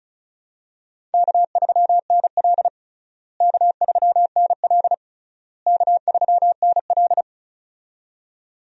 Morse code, heard as a keyed tone, about 36 wpm, 700 Hz. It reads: K3NL K3NL K3NL
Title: USA Call Signs - 4 Characters - Repeated 3x - 35wpm